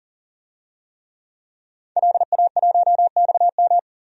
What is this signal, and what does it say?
Morse code, heard as a keyed tone, about 40 wpm, 700 Hz. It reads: LA1XM
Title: DXCC-100 Call Signs - 5 Characters - 40wpm